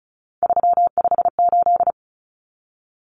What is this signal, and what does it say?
Morse code, heard as a keyed tone, about 34 wpm, 700 Hz. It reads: 358
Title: Common Cut Numbers - 3-Digit 35wpm